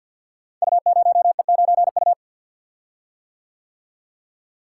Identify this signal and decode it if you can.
Morse code, heard as a keyed tone, about 52 wpm, 700 Hz. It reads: U0E9U
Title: Code Groups - 5 Letter:Numbers - 50wpm